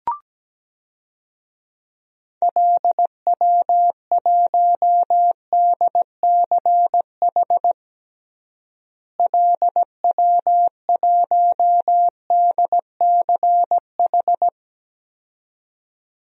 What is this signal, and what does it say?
Morse code, heard as a keyed tone, about 17 wpm, 700 Hz. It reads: LW1DCH LW1DCH
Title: DXCC-100 Call Signs - 6 Characters - Repeated 2x - 17wpm